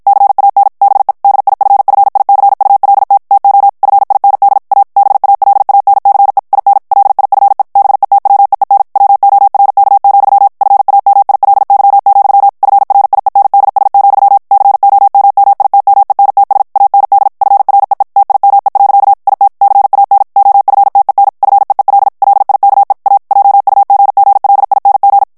Morse code, heard as a keyed tone, about 51 wpm, 800 Hz. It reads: CAN BE DIFFICULT TO FIND A BALANCE IN LIFE BETWEEN WORK, FAMILY, FRIENDS, COMMITMENTS AND FREE TIME. IT CAN OFTEN FEEL LIKE A JUGGLING
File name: morse mp3 / ANGLAIS texte clair 1 vitesse 50 mots minute